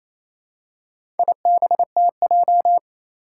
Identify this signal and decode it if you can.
Morse code, heard as a keyed tone, about 27 wpm, 700 Hz. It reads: IBTJ
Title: Four Challenging Letter-Number Combinations 28wpm